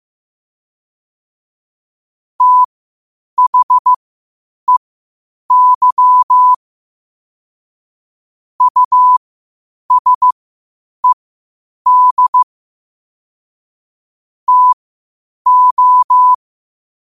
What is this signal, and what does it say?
Morse code, heard as a keyed tone, about 15 wpm, spaced as for 8 wpm, 1000 Hz. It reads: THEY USED TO
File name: qotdfiles / 8wpm